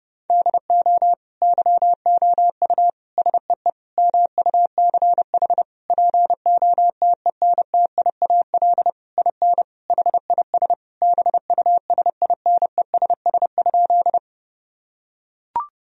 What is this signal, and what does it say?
Morse code, heard as a keyed tone, about 30 wpm, 700 Hz. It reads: DO YOU SEE MUCH POTENTIAL IN HIS BUSINESS?
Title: Sentences from Top 1000 Words - Review Part 4 - 30wpm